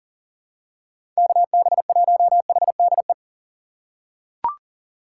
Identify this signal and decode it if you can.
Morse code, heard as a keyed tone, about 40 wpm, 700 Hz. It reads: KB1HDE
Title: USA Call Signs - 6 Characters - 4x Character Spacing - 40wpm